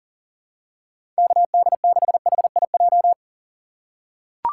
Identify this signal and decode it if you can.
Morse code, heard as a keyed tone, about 40 wpm, 700 Hz. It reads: KD6HIJ
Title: USA Call Signs - 6 Characters - 2x Character Spacing - 40wpm